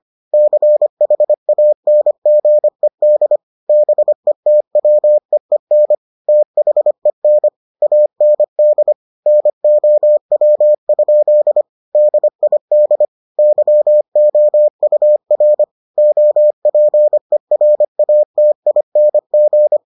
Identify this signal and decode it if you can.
Morse code, heard as a keyed tone, about 25 wpm, 600 Hz. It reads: CHANGED BETWEEN THEN AND NOW? DID YOUR OPERATING